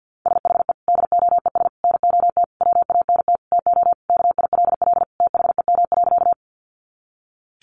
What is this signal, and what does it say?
Morse code, heard as a keyed tone, about 50 wpm, 700 Hz. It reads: SHE DOES NOT WANT TO KILL THEM.
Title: Easy Phrases 50wpm